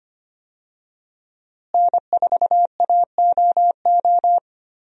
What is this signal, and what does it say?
Morse code, heard as a keyed tone, about 25 wpm, 700 Hz. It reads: N4AOO
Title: USA Call Signs - 5 Characters - Repeated 2x - 25wpm